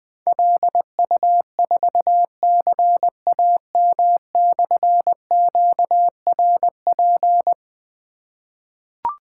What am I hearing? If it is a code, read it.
Morse code, heard as a keyed tone, about 21 wpm, 700 Hz. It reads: LU4CAM/QRP
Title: Difficult Call Signs 20wpm